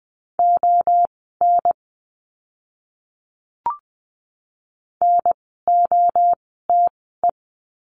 Morse code, heard as a keyed tone, about 20 wpm, spaced as for 14 wpm, 700 Hz. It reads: ON NOTE
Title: 030 Lesson 6f - Letter i - Words 20wpm